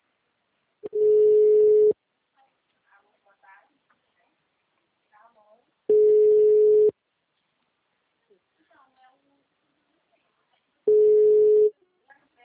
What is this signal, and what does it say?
Ringback tone (Europe, 425 Hz), with speech.